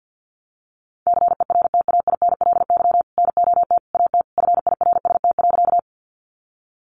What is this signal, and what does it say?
Morse code, heard as a keyed tone, about 50 wpm, 700 Hz. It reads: CERTAINLY NOT AT FIRST.
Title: Sentences from Top 900 Words 50wpm